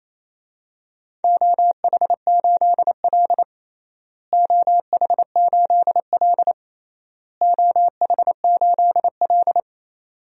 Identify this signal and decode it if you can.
Morse code, heard as a keyed tone, about 28 wpm, 700 Hz. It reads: OH8L OH8L OH8L
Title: DXCC-100 Call Signs - 4 Characters - Repeated 3x - 28wpm